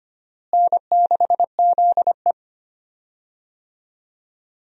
Morse code, heard as a keyed tone, about 25 wpm, 700 Hz. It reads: N6ZE
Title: USA Call Signs - 4 Characters - Repeated 3x - 25wpm